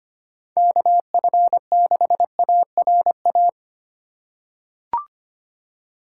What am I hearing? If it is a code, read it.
Morse code, heard as a keyed tone, about 25 wpm, 700 Hz. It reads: KF6ARA
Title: USA Call Signs - 6 Characters - 4x Character Spacing - 25wpm